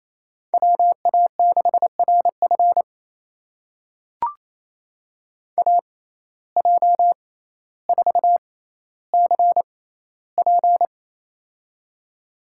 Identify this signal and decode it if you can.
Morse code, heard as a keyed tone, about 28 wpm, spaced as for 14 wpm, 700 Hz. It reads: WA6RF AJ4CP
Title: USA Call Signs - 5 Characters - 6x Character Spacing - 28wpm